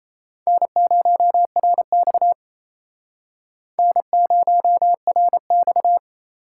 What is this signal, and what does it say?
Morse code, heard as a keyed tone, about 37 wpm, 700 Hz. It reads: N0RX N0RX